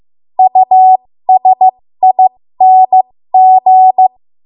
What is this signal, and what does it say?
Morse code, heard as a keyed tone, about 15 wpm, spaced as for 13 wpm, 750 Hz. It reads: USING
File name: Morse / 13WPM